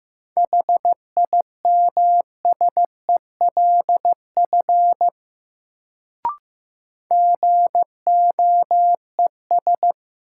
Morse code, heard as a keyed tone, about 16 wpm, 700 Hz. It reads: HIMSELF GOES
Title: Sentences from Top 1500 Words - New Word Drill Alt - 15wpm